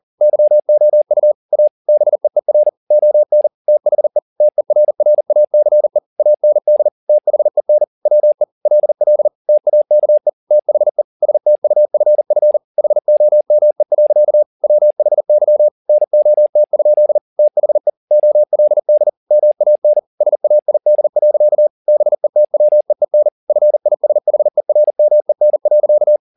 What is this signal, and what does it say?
Morse code, heard as a keyed tone, about 40 wpm, 600 Hz. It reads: YOU A BEER ON THE TERRACE AND THEN WE LL TAKE THE STUFF HOME. WHY NOT? THE OLD MAN SAID. BETWEEN FISHERMEN.